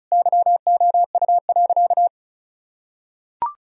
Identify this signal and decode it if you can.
Morse code, heard as a keyed tone, about 36 wpm, 700 Hz. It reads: YOU.
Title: Sentences from Top 1000 Words - Review Part 4 - 35wpm